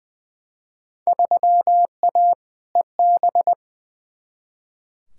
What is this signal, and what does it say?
Morse code, heard as a keyed tone, about 20 wpm, 700 Hz. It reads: 3A EB
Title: Field Day Exchanges Encore - 20wpm